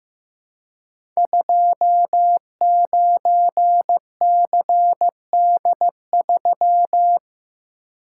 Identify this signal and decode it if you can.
Morse code, heard as a keyed tone, about 15 wpm, 700 Hz. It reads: 29CD3
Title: Code Groups - 5 Letter:Numbers - 15wpm